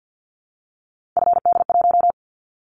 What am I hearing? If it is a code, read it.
Morse code, heard as a keyed tone, about 50 wpm, 700 Hz. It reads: FD1
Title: Callsign Prefixes for Top 50 DXCC - 50wpm